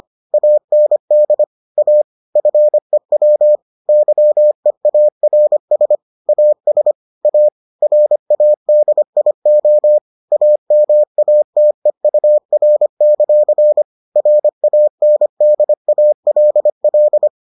Morse code, heard as a keyed tone, about 25 wpm, 600 Hz. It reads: AND A FEW YEARS AS A RADIO AMATEUR; RANDALL